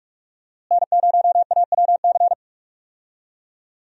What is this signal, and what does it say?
Morse code, heard as a keyed tone, about 46 wpm, 700 Hz. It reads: N0AWC
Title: USA Call Signs - 5 Characters - Repeated 2x - 45wpm